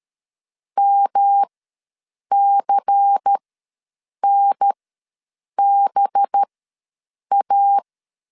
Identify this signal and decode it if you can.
Morse code, heard as a keyed tone, about 13 wpm, spaced as for 7 wpm, 800 Hz. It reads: MCNBA
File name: LEZIONI / 6 PKX0 / 6 PKX ws 12wpm 3106